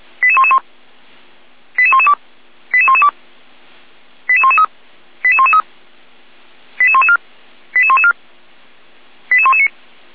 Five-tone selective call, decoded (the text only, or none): 88111 88112 88112 88113 88113 88114 88114 88119